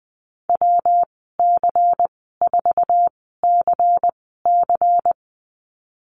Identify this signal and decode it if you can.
Morse code, heard as a keyed tone, about 20 wpm, spaced as for 15 wpm, 700 Hz. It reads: WC4CC
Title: 154 Lesson 25g - Letter p - Call Signs 20wpm